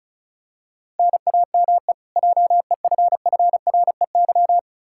Morse code, heard as a keyed tone, about 36 wpm, 700 Hz. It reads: NAME JEFFREY
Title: Practice QSO - Exchange 1 - Beginning - 35wpm